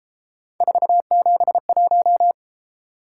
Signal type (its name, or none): Morse code